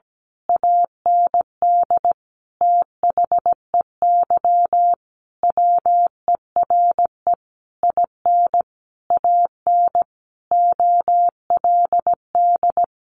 Morse code, heard as a keyed tone, about 17 wpm, 700 Hz. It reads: AND THEY WERE IN AN OLD
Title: A Sound Of Thunder 17wpm